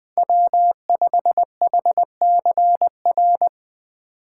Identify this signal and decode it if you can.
Morse code, heard as a keyed tone, about 21 wpm, 700 Hz. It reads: W5HCR
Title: USA Call Signs - 5 Characters - 2x Character Spacing - 20wpm